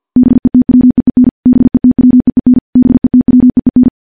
Morse code, heard as a keyed tone, about 50 wpm, 250 Hz. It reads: BETWEEN BETWEEN BETWEEN
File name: morse / morse-training-50WPM